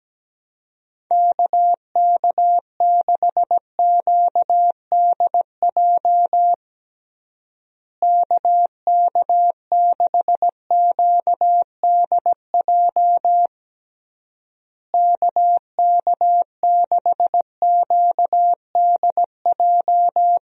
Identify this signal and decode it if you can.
Morse code, heard as a keyed tone, about 17 wpm, 700 Hz. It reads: KK6QDJ KK6QDJ KK6QDJ
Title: USA Call Signs - 6 Characters - Repeated 3x - 17wpm